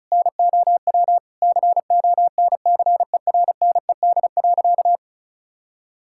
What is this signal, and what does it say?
Morse code, heard as a keyed tone, about 36 wpm, 700 Hz. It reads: NOW CONCERNED.
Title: Sentences from Top 1000 Words - Review Part 6 - 35wpm